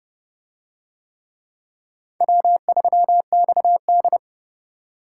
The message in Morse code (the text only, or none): W3XD